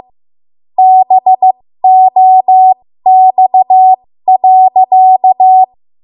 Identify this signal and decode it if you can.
Morse code, heard as a keyed tone, about 15 wpm, spaced as for 13 wpm, 750 Hz. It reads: BOX.
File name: Morse / 13WPM